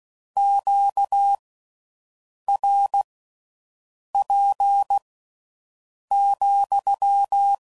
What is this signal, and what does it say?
Morse code, heard as a keyed tone, about 16 wpm, spaced as for 6 wpm, 800 Hz. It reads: QRP,